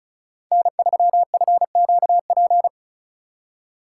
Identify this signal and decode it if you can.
Morse code, heard as a keyed tone, about 36 wpm, 700 Hz. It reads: N3FQP
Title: USA Call Signs - 5 Characters - 2x Character Spacing - 35wpm